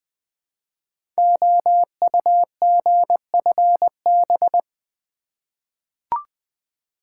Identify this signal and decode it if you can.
Morse code, heard as a keyed tone, about 20 wpm, 700 Hz. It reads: OUGFB
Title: Code Groups - 5 Letters - 20wpm